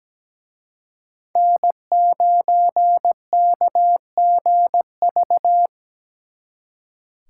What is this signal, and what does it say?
Morse code, heard as a keyed tone, about 20 wpm, 700 Hz. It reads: N9KGV
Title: Speed Racing - USA 1x3 Call Signs 17wpm